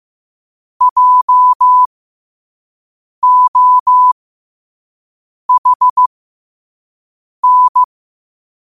Morse code, heard as a keyed tone, about 15 wpm, spaced as for 5 wpm, 1000 Hz. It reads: JOHN
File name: qotdfiles / 5wpm